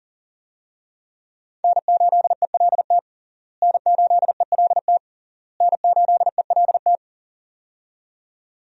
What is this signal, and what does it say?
Morse code, heard as a keyed tone, about 40 wpm, 700 Hz. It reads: N8ELT N8ELT N8ELT
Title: USA Call Signs - 5 Characters - Repeated 3x - 40wpm